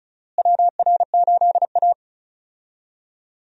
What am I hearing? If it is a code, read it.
Morse code, heard as a keyed tone, about 36 wpm, 700 Hz. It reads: WR8A
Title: USA Call Signs - 4 Characters - Repeated 2x - 35wpm